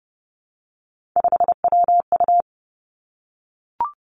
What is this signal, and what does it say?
Morse code, heard as a keyed tone, about 30 wpm, 700 Hz. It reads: 5WU
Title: Three Challenging Letter-Number Combinations 30wpm